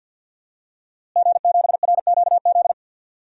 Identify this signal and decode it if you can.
Morse code, heard as a keyed tone, about 46 wpm, 700 Hz. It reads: G7RQZ
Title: DXCC-100 Call Signs - 5 Characters - Repeated 2x - 50wpm